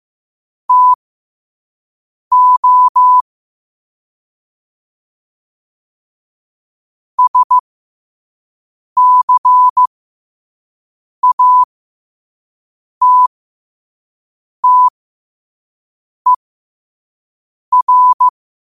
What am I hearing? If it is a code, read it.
Morse code, heard as a keyed tone, about 15 wpm, spaced as for 5 wpm, 1000 Hz. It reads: TO SCATTER
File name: qotdfiles / 5wpm